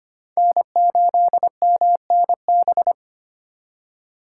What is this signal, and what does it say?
Morse code, heard as a keyed tone, about 33 wpm, 700 Hz. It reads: N8MNB